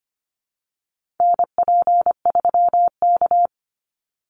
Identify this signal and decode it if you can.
Morse code, heard as a keyed tone, about 25 wpm, 700 Hz. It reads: NP3K
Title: Call Signs Repeated 3x - 25wpm